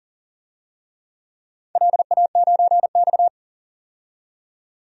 Morse code, heard as a keyed tone, about 40 wpm, 700 Hz. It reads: LA9X
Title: DXCC-100 Call Signs - 4 Characters - 40wpm